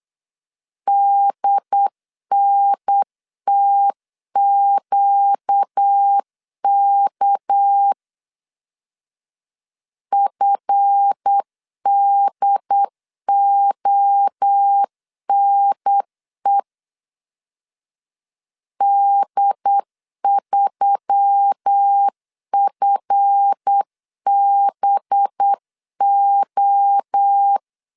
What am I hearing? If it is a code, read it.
Morse code, heard as a keyed tone, about 8 wpm, 800 Hz. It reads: DNTQK FDONE D3FBO
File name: LEZIONI / 7 GF9Z / 7 GF 8wpm 3108